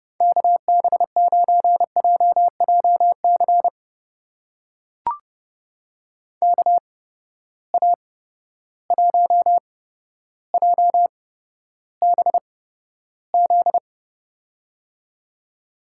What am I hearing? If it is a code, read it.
Morse code, heard as a keyed tone, about 30 wpm, spaced as for 13 wpm, 700 Hz. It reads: KB9JJC KA1JBZ
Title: USA Call Signs - 6 Characters - 8x Character Spacing - 30wpm